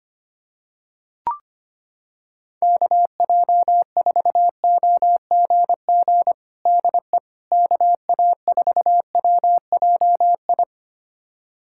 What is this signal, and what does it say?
Morse code, heard as a keyed tone, about 25 wpm, 700 Hz. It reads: KJ4OGG DE KA4WJI
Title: USA Callsign DE Callsign - 6 Characters - 25wpm